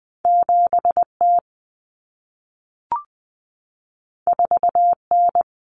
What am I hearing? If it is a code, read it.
Morse code, heard as a keyed tone, about 20 wpm, 700 Hz. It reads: 7T 4N
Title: Common Cut Numbers - 2-Digit 20wpm